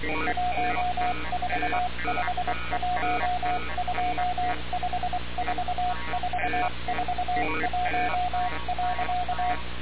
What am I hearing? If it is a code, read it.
Morse code, heard as a keyed tone, about 23 wpm, 720 Hz, with noise. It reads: 1451254441.
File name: radiocap / REC z